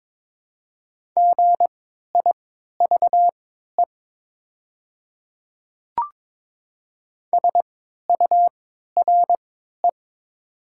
Morse code, heard as a keyed tone, about 22 wpm, spaced as for 12 wpm, 700 Hz. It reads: GIVE SURE
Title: 3x Character Spacing - Top 300 Words - 22wpm